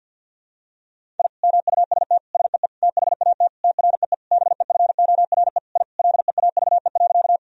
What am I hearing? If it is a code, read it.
Morse code, heard as a keyed tone, about 50 wpm, 700 Hz. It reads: I MUST SEE THAT TREE BEFORE I LEAVE.
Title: Sentences from Top 1000 Words - Review Part 1 - 50wpm